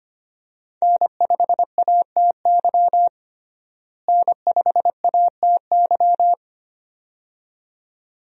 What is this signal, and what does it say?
Morse code, heard as a keyed tone, about 25 wpm, 700 Hz. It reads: N5ATY N5ATY
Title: USA Call Signs - 5 Characters - Repeated 2x - 25wpm